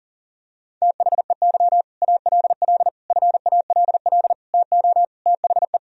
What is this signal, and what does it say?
Morse code, heard as a keyed tone, about 40 wpm, 700 Hz. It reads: THEY ALL FALL TO THE
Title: Sentences from Top 500 Words and Abbreviations 40wpm